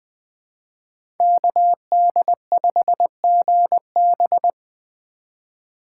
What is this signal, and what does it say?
Morse code, heard as a keyed tone, about 20 wpm, 700 Hz. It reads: KD5GB
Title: Speed Racing - USA Call Signs 17wpm